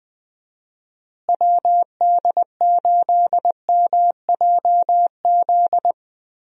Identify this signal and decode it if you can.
Morse code, heard as a keyed tone, about 20 wpm, 700 Hz. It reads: WD8MJZ
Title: USA Call Signs - 6 Characters - Repeated 3x - 20wpm